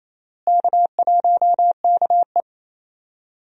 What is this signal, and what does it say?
Morse code, heard as a keyed tone, about 28 wpm, 700 Hz. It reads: K1KE
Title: USA Call Signs - 4 Characters - Repeated 3x - 28wpm